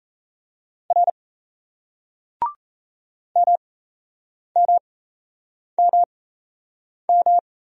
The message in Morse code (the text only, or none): R MMMM